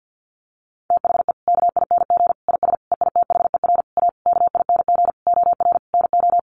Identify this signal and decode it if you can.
Morse code, heard as a keyed tone, about 50 wpm, 700 Hz. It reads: THE KING IS EITHER A KING OR NO
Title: Sentences from Top 600 Words 50wpm